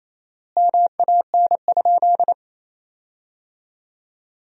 Morse code, heard as a keyed tone, about 28 wpm, 700 Hz. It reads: MAN?
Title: 4x Word-Spacing - Sentences from Top 200 Words - 28wpm